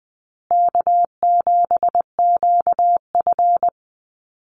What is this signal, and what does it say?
Morse code, heard as a keyed tone, about 20 wpm, 700 Hz. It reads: K7QF
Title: USA Call Signs Repeated 2x 20wpm